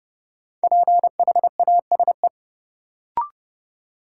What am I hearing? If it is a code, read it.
Morse code, heard as a keyed tone, about 30 wpm, 700 Hz. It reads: PHASE
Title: Sentences from Top 1200 Words - New Word Drill Alt - 30wpm